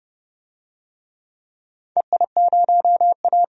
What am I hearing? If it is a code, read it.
Morse code, heard as a keyed tone, about 30 wpm, 700 Hz. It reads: EI0A
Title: DXCC-100 Call Signs - 4 Characters - 30wpm